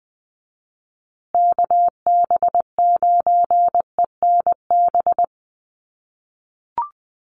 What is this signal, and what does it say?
Morse code, heard as a keyed tone, about 20 wpm, 700 Hz. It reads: KB9ENB
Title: USA Call Signs Repeated 3x 20wpm